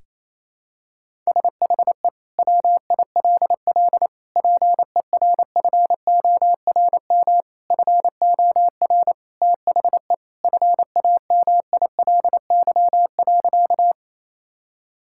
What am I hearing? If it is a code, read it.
Morse code, heard as a keyed tone, about 28 wpm, 700 Hz. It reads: SHE WILL PERFORM FOR THE FAMILY.
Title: Sentences from Top 1000 Words - Review Part 5 - 28wpm